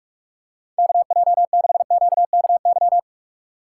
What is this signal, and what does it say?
Morse code, heard as a keyed tone, about 46 wpm, 700 Hz. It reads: KJ6QKY
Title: USA Call Signs - 6 Characters - 6x Character Spacing - 45wpm